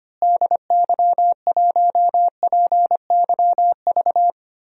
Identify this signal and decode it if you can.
Morse code, heard as a keyed tone, about 25 wpm, 700 Hz. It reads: DY1PYV
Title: DXCC-100 Call Signs - 6 Characters - Repeated 3x - 25wpm